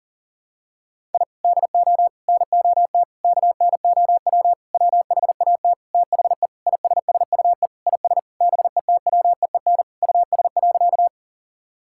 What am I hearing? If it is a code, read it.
Morse code, heard as a keyed tone, about 40 wpm, 700 Hz. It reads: I DO NOT KNOW WHAT THE ISSUE IS BETWEEN US.
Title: Sentences from Top 1000 Words - Review Part 6 - 40wpm